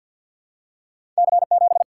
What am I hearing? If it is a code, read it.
Morse code, heard as a keyed tone, about 52 wpm, 700 Hz. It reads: C7
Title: Single-Letter Single-Number 50wpm